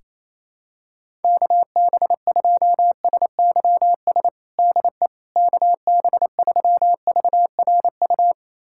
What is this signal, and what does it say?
Morse code, heard as a keyed tone, about 27 wpm, 700 Hz. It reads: KB2SYS DE KB3VRU
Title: USA Callsign DE Callsign - 6 Characters - 28wpm